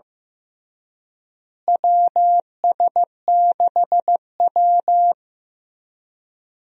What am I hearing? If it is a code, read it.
Morse code, heard as a keyed tone, about 15 wpm, 700 Hz. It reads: WS6W
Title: USA Call Signs - 4 Characters - Repeated 3x - 15wpm